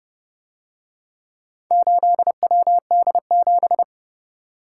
Morse code, heard as a keyed tone, about 29 wpm, 700 Hz. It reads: 8WD7